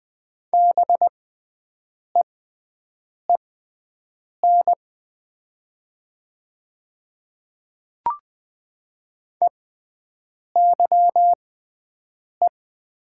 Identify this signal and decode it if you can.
Morse code, heard as a keyed tone, about 20 wpm, spaced as for 7 wpm, 700 Hz. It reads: BEEN EYE
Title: 6x Character Spacing - Top 300 Words - 20wpm